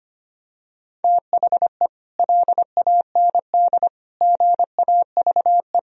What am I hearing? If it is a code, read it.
Morse code, heard as a keyed tone, about 25 wpm, 700 Hz. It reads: THE LAND GAVE
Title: Sentences from Top 1000 Words - Review Part 1 - 25wpm